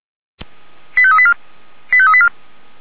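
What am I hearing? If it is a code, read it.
Five-tone selective call (ZVEI), heard as quick runs of five tones, tones of about 70 ms. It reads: 85384 85384
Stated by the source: ZVEI five-tone selective call received on an SDR.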